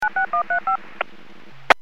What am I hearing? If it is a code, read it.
DTMF, heard as telephone touch tones, tones of about 110 ms. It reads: 96135
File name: WPS / 00229 13-16 WPS 451.25 WPS